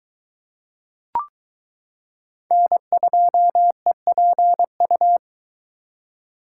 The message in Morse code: N2EPU

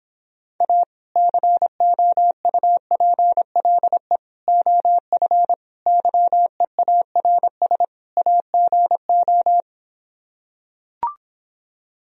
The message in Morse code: A COUPLE OF YEARS AGO

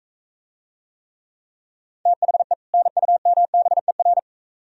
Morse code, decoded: THE NUMBER

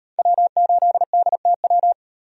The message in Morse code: W8DTW

